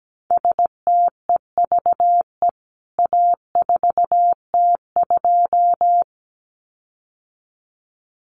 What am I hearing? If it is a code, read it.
Morse code, heard as a keyed tone, about 17 wpm, 700 Hz. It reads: STEVE A4T2
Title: CWT Exchange 17wpm